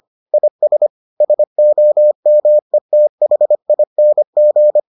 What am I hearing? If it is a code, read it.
Morse code, heard as a keyed tone, about 25 wpm, 600 Hz. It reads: IS SOMETHING